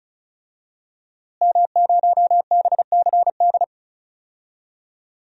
Morse code, heard as a keyed tone, about 34 wpm, 700 Hz. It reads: M0BCD
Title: DXCC-100 Call Signs - 5 Characters - Repeated 3x - 35wpm